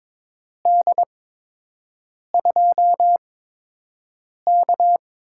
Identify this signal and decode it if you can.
Morse code, heard as a keyed tone, about 22 wpm, spaced as for 6 wpm, 700 Hz. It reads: D2K
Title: USA Call Signs - 8x Character Spacing - 22wpm